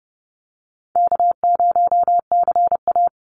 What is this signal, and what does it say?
Morse code, heard as a keyed tone, about 30 wpm, 700 Hz. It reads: K0CA
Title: Easy Call Signs - 30wpm